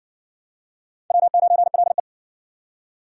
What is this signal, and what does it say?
Morse code, heard as a keyed tone, about 60 wpm, 700 Hz. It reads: W9LE